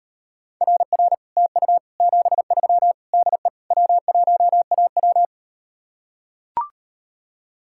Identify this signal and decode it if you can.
Morse code, heard as a keyed tone, about 38 wpm, 700 Hz. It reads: RR TU 73 DE W1AW